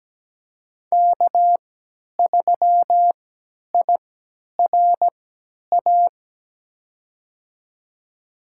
Morse code, heard as a keyed tone, about 17 wpm, spaced as for 10 wpm, 700 Hz. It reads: K3IRA
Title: USA Call Signs - 5 Characters - 3x Character Spacing - 17wpm